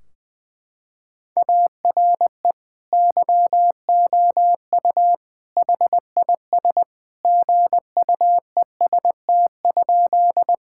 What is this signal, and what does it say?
Morse code, heard as a keyed tone, about 21 wpm, 700 Hz. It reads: ARE YOU HIS GUEST?
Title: Sentences from Top 1600 Words 20wpm